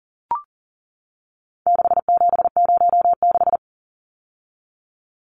Morse code, heard as a keyed tone, about 40 wpm, 700 Hz. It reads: 6706
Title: Numbers - 4-Digit 40wpm